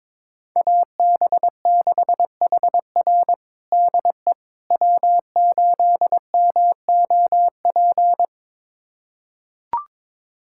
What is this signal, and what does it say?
Morse code, heard as a keyed tone, about 22 wpm, 700 Hz. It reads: AB6HR DE W8MOP
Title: USA Callsign DE Callsign - 5 Characters - 22wpm